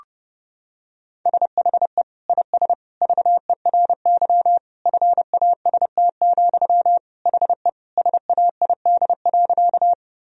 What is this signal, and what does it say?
Morse code, heard as a keyed tone, about 32 wpm, 700 Hz. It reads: SHE IS VERY FAST, HE SAID.
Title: Sentences from Top 1000 Words - Review Part 2 - 30wpm